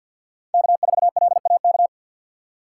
Morse code, heard as a keyed tone, about 51 wpm, 700 Hz. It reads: K4LAK
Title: USA Call Signs - 5 Characters - 3x Character Spacing - 50wpm